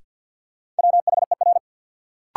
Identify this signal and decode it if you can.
Morse code, heard as a keyed tone, about 53 wpm, 700 Hz. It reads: WHER
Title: Sentences from Top 1500 Words - New Word Drill Alt - 50wpm